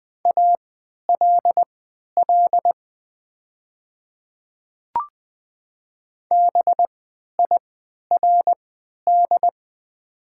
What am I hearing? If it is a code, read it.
Morse code, heard as a keyed tone, about 20 wpm, spaced as for 11 wpm, 700 Hz. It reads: ALL BIRD
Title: 3x Character Spacing - Top 500 Words - 20wpm